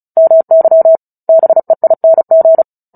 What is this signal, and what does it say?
Morse code, heard as a keyed tone, about 34 wpm, 650 Hz. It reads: MY BEING
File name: Book Two / 09-Chapter 9 - 35 WPM